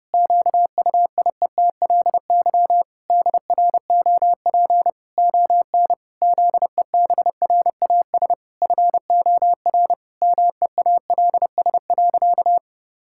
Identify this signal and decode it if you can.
Morse code, heard as a keyed tone, about 30 wpm, 700 Hz. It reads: QUIETLY DROP ON ZEBRAS FOR MEALS.
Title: Pangram Challenge v2 - 30wpm